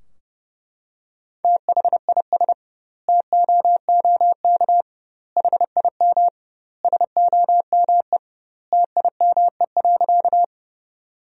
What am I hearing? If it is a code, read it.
Morse code, heard as a keyed tone, about 30 wpm, 700 Hz. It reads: THIS TOOK HIM SOME TIME.
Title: 2x Word-Spacing - Sentences from Top 300 Words - 30wpm